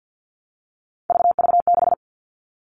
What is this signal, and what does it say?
Morse code, heard as a keyed tone, about 48 wpm, 700 Hz. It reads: VV6
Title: All Cut Numbers - 3-Digit 50wpm